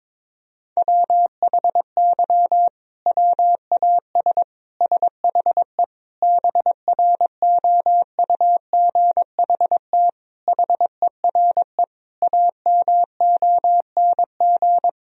Morse code, heard as a keyed tone, about 22 wpm, 700 Hz. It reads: WHY WAS SHE BROUGHT HERE AMONG
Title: Sentences from Top 1000 Words - Review Part 3 - 22wpm